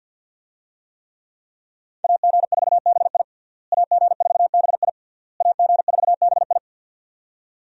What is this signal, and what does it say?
Morse code, heard as a keyed tone, about 48 wpm, 690 Hz. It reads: AG4BI AG4BI AG4BI
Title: USA Call Signs - 5 Characters - Repeated 3x - 50wpm